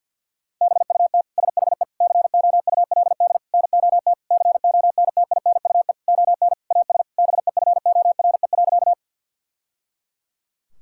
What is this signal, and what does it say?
Morse code, heard as a keyed tone, about 50 wpm, 700 Hz. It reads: BUT SHE COULD NOT CONTINUE ON AS BEFORE.
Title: Sentences from Top 1000 Words - Review Part 6 - 50wpm